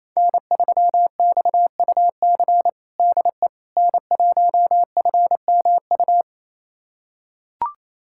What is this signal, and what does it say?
Morse code, heard as a keyed tone, about 28 wpm, 700 Hz. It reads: N3XUC DE N1FMU
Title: USA Callsign DE Callsign - 5 Characters - 28wpm